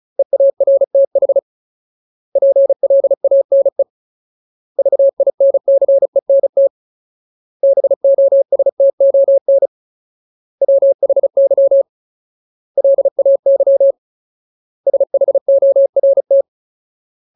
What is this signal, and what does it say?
Morse code, heard as a keyed tone, about 34 wpm, 550 Hz. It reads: EARTH PLANE VINCENT BOSTON WHY LAY SHORT
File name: Ultimate Word List / 35WPM Ultimate